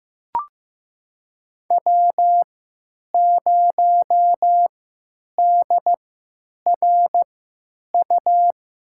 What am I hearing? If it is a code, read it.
Morse code, heard as a keyed tone, about 15 wpm, spaced as for 9 wpm, 700 Hz. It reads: W0DRU
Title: USA Call Signs - 5 Characters - 3x Character Spacing - 15wpm